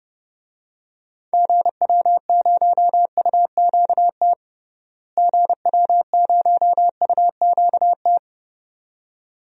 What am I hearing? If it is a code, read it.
Morse code, heard as a keyed tone, about 30 wpm, 700 Hz. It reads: GW0UQT GW0UQT
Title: DXCC-100 Call Signs - 6 Characters - Repeated 2x - 30wpm